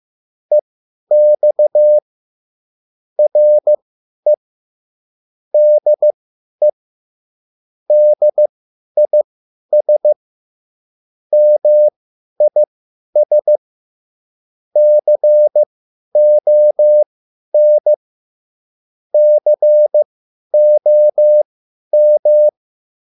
Morse code, heard as a keyed tone, about 15 wpm, spaced as for 10 wpm, 600 Hz. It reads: EX RE DE DIS MIS CON COM